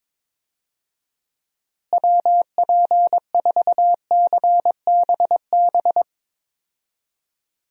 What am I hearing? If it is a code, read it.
Morse code, heard as a keyed tone, about 22 wpm, 700 Hz. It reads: WP4CBB
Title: USA Call Signs - 6 Characters - Repeated 3x - 22wpm